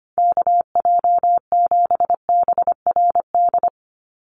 Morse code, heard as a keyed tone, about 25 wpm, 700 Hz. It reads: KJ7BRD
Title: USA Call Signs 25wpm